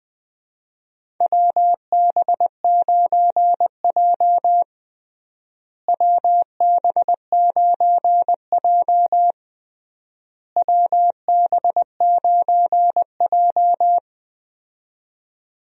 Morse code, heard as a keyed tone, about 20 wpm, 700 Hz. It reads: WB9J WB9J WB9J
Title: USA Call Signs - 4 Characters - Repeated 3x - 20wpm